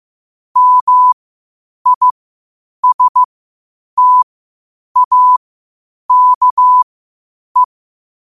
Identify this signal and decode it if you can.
Morse code, heard as a keyed tone, about 15 wpm, spaced as for 8 wpm, 1000 Hz. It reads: MISTAKE
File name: 8wpm